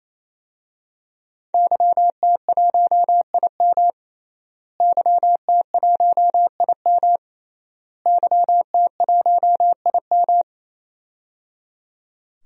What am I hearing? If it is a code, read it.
Morse code, heard as a keyed tone, about 28 wpm, 700 Hz. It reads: YT1IM YT1IM YT1IM
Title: DXCC-100 Call Signs - 5 Characters - Repeated 3x - 28wpm